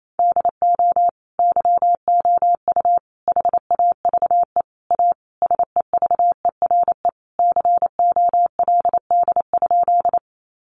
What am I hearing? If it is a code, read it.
Morse code, heard as a keyed tone, about 28 wpm, 700 Hz. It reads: DO YOU HAVE A SEVERE COLD?
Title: Sentences from Top 1400 Words 28wpm